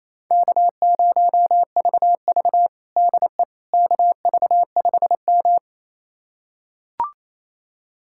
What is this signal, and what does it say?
Morse code, heard as a keyed tone, about 28 wpm, 700 Hz. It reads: K0VV DE KV5M
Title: USA Callsign DE Callsign - 4 Characters - 28wpm